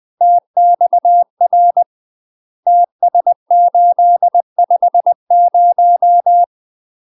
Morse code, heard as a keyed tone, about 20 wpm, 700 Hz. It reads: TXR TS850
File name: UK MASTER QSOs TRAINING / 20WPM / UK MASTER QSOs 0006